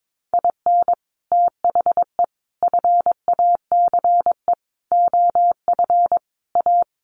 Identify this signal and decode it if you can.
Morse code, heard as a keyed tone, about 22 wpm, 700 Hz. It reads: IN THE FACE OF A
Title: Sets of 5 Words Encore 22wpm